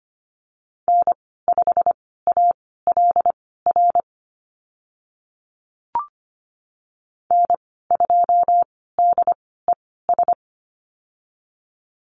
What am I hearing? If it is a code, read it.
Morse code, heard as a keyed tone, about 25 wpm, spaced as for 16 wpm, 700 Hz. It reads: N5ALR N2DES
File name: Lesson 15g - Number 5 - Call Signs 25wpm